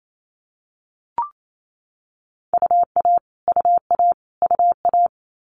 Morse code, heard as a keyed tone, about 28 wpm, 700 Hz. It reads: UA UA UA
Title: Multi-letter Phonemes Repeated 3x 28wpm